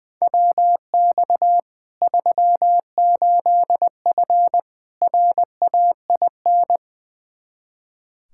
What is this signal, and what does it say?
Morse code, heard as a keyed tone, about 21 wpm, 700 Hz. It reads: WX 38F RAIN
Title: Practice QSO - Exchange 2 - Info - 20wpm